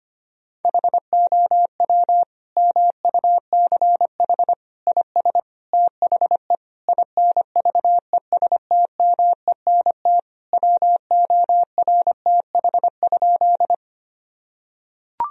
Morse code, heard as a keyed tone, about 25 wpm, 700 Hz. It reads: HOW MUCH IS THE INVESTMENT WORTH?
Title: Sentences from Top 1000 Words - Review Part 4 - 25wpm